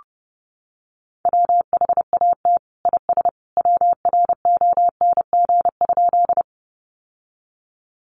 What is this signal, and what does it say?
Morse code, heard as a keyed tone, about 30 wpm, 700 Hz. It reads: WHAT IS WRONG?
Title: Sentences from Top 900 Words 30wpm